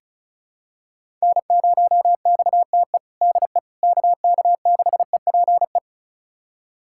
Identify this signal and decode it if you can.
Morse code, heard as a keyed tone, about 36 wpm, 700 Hz. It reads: N0XTE DE KK6EPE
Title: USA Callsign DE Callsign 35wpm